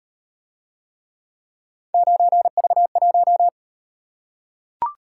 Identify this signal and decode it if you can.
Morse code, heard as a keyed tone, about 37 wpm, 700 Hz. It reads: 9V1